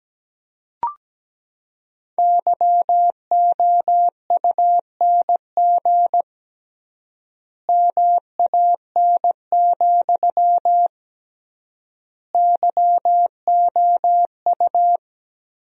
Morse code, heard as a keyed tone, about 17 wpm, 700 Hz. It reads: YOUNG MAN, YOU
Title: 3x Word-Spacing - Sentences from Top 400 Words - 17wpm